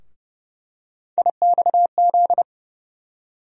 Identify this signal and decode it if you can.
Morse code, heard as a keyed tone, about 30 wpm, 700 Hz. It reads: IXZ
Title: USA Call Sign Postfixes - 3 Letters 30wpm